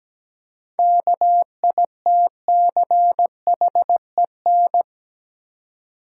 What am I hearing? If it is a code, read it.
Morse code, heard as a keyed tone, about 17 wpm, 700 Hz. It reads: KITCHEN
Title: Seven Letter Words - 17wpm